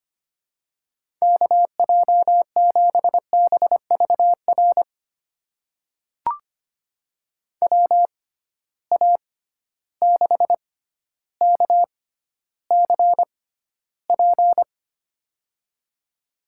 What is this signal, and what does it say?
Morse code, heard as a keyed tone, about 25 wpm, spaced as for 13 wpm, 700 Hz. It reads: KJ7BVR WA6KCP
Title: USA Call Signs - 6 Characters - 6x Character Spacing - 25wpm